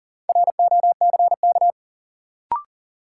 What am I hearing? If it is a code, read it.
Morse code, heard as a keyed tone, about 39 wpm, 700 Hz. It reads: ROCK
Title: Sentences from Top 400 Words - New Word Drill - 40wpm